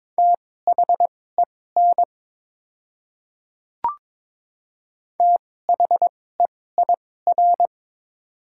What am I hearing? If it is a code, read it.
Morse code, heard as a keyed tone, about 22 wpm, spaced as for 16 wpm, 700 Hz. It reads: THEN THEIR
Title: 2x Character Spacing - Top 100 Words - 22wpm